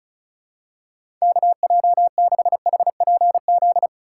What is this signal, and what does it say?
Morse code, heard as a keyed tone, about 34 wpm, 700 Hz. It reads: KJ6HPZ
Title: USA Call Signs - 6 Characters - 3x Character Spacing - 35wpm